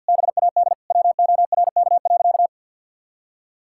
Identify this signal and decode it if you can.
Morse code, heard as a keyed tone, about 52 wpm, 700 Hz. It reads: BAD WORK.
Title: Sentences from Top 1000 Words - Review Part 6 - 50wpm